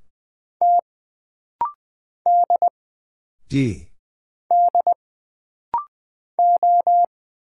Morse code, heard as a keyed tone, about 20 wpm, 700 Hz. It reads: T D D O
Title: Single Letters - Rapid-Fire 20wpm